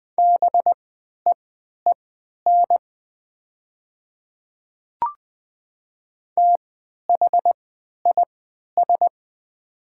Morse code, heard as a keyed tone, about 20 wpm, spaced as for 11 wpm, 700 Hz. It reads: BEEN THIS
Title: 3x Character Spacing - Top 100 Words - 20wpm